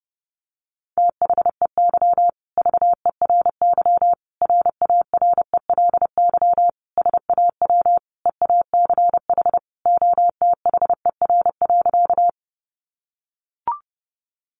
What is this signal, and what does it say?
Morse code, heard as a keyed tone, about 30 wpm, 700 Hz. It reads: THEY VERY RARELY SAW EACH OTHER.
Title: Sentences from Top 1400 Words 30wpm